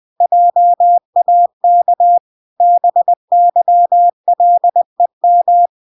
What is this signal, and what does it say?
Morse code, heard as a keyed tone, about 20 wpm, 700 Hz. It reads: JAK BYLEM